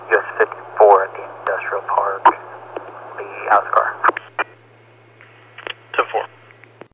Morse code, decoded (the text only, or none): none